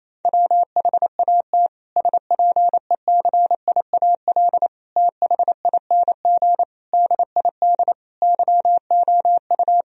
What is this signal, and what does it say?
Morse code, heard as a keyed tone, about 28 wpm, 700 Hz. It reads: WHAT SPECIAL THING DID YOU
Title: Sentences from Top 1000 Words - Review Part 3 - 28wpm